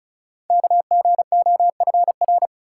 Morse code, heard as a keyed tone, about 35 wpm, 690 Hz. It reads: KGOFR
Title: Code Groups - 5 Letters - 35wpm